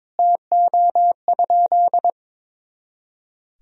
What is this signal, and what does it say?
Morse code, heard as a keyed tone, about 22 wpm, 700 Hz. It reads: TO?
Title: Sentences from Top 200 Words and Abbreviations 22wpm